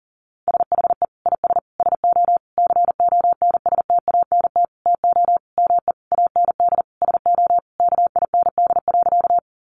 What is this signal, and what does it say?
Morse code, heard as a keyed tone, about 40 wpm, 700 Hz. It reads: SHE IS SO CONSTANT TO ME AND SO KIND.
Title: Sentences from Top 1500 Words 40wpm